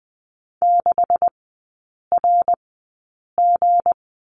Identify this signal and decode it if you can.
Morse code, heard as a keyed tone, about 20 wpm, spaced as for 8 wpm, 690 Hz. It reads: 6RG
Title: 159 Lesson 26e - Letter g - Three Characters 20wpm